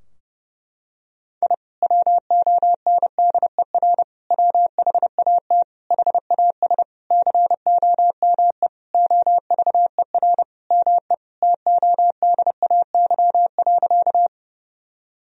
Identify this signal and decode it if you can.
Morse code, heard as a keyed tone, about 32 wpm, 700 Hz. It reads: I WONDER WHAT HAS COME OVER ME TODAY.
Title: Sentences from Top 1000 Words - Review Part 4 - 30wpm